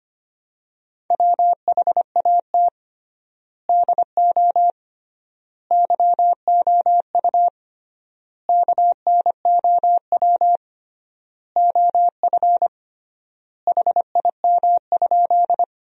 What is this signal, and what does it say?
Morse code, heard as a keyed tone, about 25 wpm, 700 Hz. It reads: WHAT DO YOU KNOW OF HIM?
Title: 3x Word-Spacing - Sentences from Top 100 Words - 25wpm